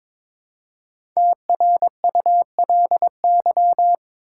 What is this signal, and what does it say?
Morse code, heard as a keyed tone, about 22 wpm, 690 Hz. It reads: TRULY
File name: 5 Letter Words - First 3 Letters Given - 22wpm